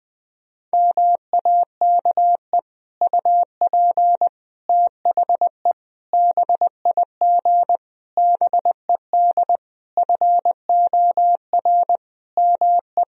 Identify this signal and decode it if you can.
Morse code, heard as a keyed tone, about 20 wpm, 700 Hz. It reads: MAKE UP THE BIG BED FOR ME
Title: Sentences from Top 1000 Words - Review Part 1 - 20wpm